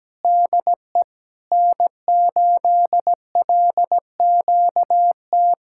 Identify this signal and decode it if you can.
Morse code, heard as a keyed tone, about 17 wpm, 700 Hz. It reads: DE N8LQT
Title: Practice QSO - Exchange 4 - Ending - 17wpm